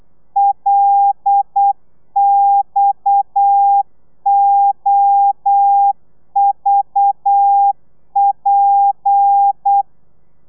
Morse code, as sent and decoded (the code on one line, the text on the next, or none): .-.. -..- --- ...- .--.
LXOVP